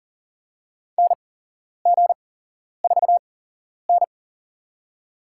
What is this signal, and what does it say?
Morse code, heard as a keyed tone, about 40 wpm, spaced as for 11 wpm, 700 Hz. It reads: NG4N
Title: USA Call Signs - 4 Characters - 8x Character Spacing - 40wpm